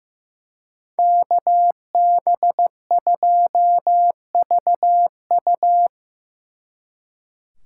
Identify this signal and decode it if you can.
Morse code, heard as a keyed tone, about 18 wpm, 700 Hz. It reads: KB2VU
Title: Speed Racing - USA 2x2 Call Signs 15wpm